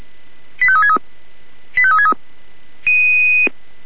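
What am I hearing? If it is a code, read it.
Five-tone selective call (ZVEI), heard as quick runs of five tones, tones of about 70 ms. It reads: 85363 85363
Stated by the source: Five-tone ZVEI selective call received on an SDR.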